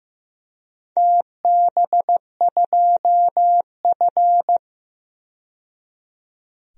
Morse code, heard as a keyed tone, about 15 wpm, 700 Hz. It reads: TB2F
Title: DXCC-100 Call Signs - 4 Characters - 15wpm